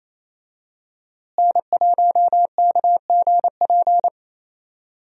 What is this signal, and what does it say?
Morse code, heard as a keyed tone, about 27 wpm, 700 Hz. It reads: N1KGP
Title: USA Call Signs - 5 Characters - Repeated 3x - 28wpm